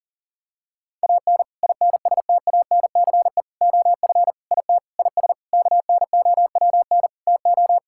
Morse code, heard as a keyed tone, about 40 wpm, 700 Hz. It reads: AN INSTANCE OF IT IS KNOWN TO
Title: Sentences from Top 1000 Words - Review Part 5 - 40wpm